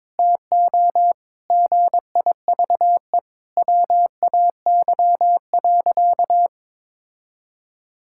Morse code, heard as a keyed tone, about 22 wpm, 700 Hz. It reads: TO GIVE WAY.
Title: Sentences from Top 1600 Words 22wpm